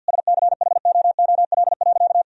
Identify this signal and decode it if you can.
Morse code, heard as a keyed tone, about 47 wpm, 700 Hz. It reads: SCHOOL.
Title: Sentences from Top 1000 Words - Review Part 2 - 50wpm